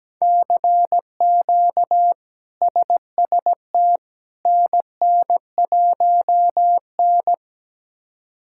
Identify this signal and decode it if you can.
Morse code, heard as a keyed tone, about 17 wpm, 700 Hz. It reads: CQ SST NN1N
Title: K1USN SST QSO Practice - 17wpm